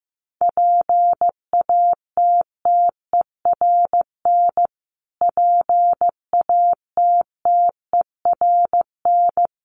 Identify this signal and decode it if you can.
Morse code, heard as a keyed tone, about 15 wpm, 700 Hz. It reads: PATTERN PATTERN
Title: Top 500 Words Repeated 15wpm